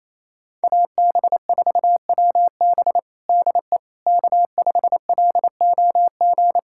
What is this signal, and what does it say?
Morse code, heard as a keyed tone, about 28 wpm, 700 Hz. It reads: AB4WB DE K5LOG
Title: USA Callsign DE Callsign - 5 Characters - 28wpm